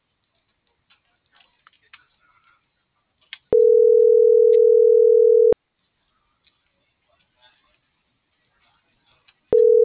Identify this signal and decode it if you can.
Ringback tone (North America), 440 + 480 Hz.